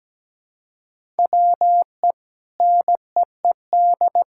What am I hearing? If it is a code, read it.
Morse code, heard as a keyed tone, about 17 wpm, 700 Hz. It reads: WE NEED
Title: Sentences from Top 1000 Words - Review Part 5 - 17wpm